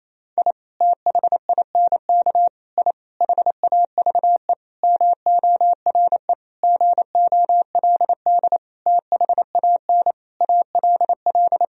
Morse code, heard as a keyed tone, about 27 wpm, 700 Hz. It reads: I THINK I HAVE MORE GOLD THAN ALL
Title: Sentences from Top 1000 Words - Review Part 3 - 28wpm